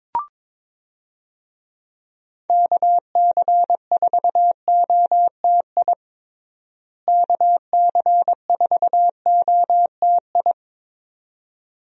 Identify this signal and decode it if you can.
Morse code, heard as a keyed tone, about 22 wpm, 700 Hz. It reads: KC4OTI KC4OTI
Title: USA Call Signs - 6 Characters - Repeated 2x - 22wpm